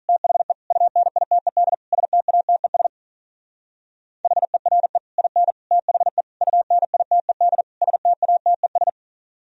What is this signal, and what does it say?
Morse code, heard as a keyed tone, about 43 wpm, 700 Hz. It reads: THE UNITED STATES HERE IN THE UNITED STATES
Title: Speed-Racing - Sets of 5 Words 35wpm